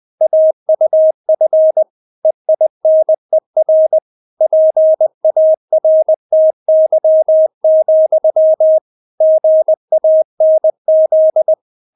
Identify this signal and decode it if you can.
Morse code, heard as a keyed tone, about 20 wpm, 630 Hz. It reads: AUF EINER PARTY, GANZ
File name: Witz1 20wpm